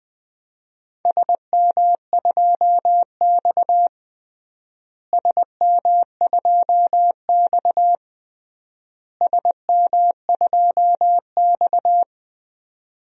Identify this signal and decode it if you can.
Morse code, heard as a keyed tone, about 20 wpm, 700 Hz. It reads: SM2X SM2X SM2X
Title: DXCC-100 Call Signs - 4 Characters - Repeated 3x - 20wpm